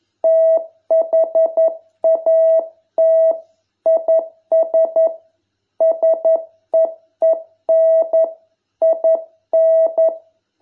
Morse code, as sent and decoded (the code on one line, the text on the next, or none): - .... .- - / .. ... / ... . -. / .. -.
THAT IS SEEN IN